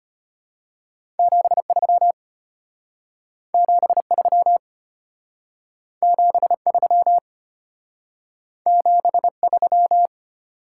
Morse code, heard as a keyed tone, about 31 wpm, 690 Hz. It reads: 73 73 73 73
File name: practice / 25wpm / Speed-Racing Top 20 QSO Elements - 25wpm